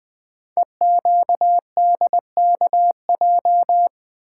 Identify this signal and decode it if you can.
Morse code, heard as a keyed tone, about 20 wpm, 700 Hz. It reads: EQDKJ
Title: Code Groups - 5 Letters - 20wpm